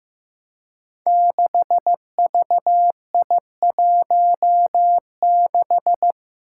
Morse code, heard as a keyed tone, about 15 wpm, 700 Hz. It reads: 6VI16
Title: Five Challenging Letter-Number Combinations 15wpm